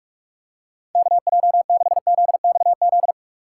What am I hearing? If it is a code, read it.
Morse code, heard as a keyed tone, about 36 wpm, 700 Hz. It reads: KJ6ZXZ
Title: Speed Racing - USA 2x3 Call Signs 30wpm